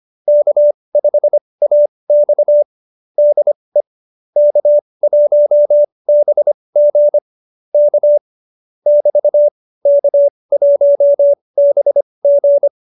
Morse code, heard as a keyed tone, about 25 wpm, spaced as for 20 wpm, 590 Hz. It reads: K5AX DE K1BG K <BT> K1BG